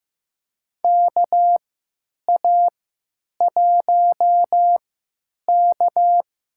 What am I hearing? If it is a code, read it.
Morse code, heard as a keyed tone, about 15 wpm, spaced as for 9 wpm, 700 Hz. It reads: KA1K
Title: USA Call Signs - 3x Character Spacing - 15wpm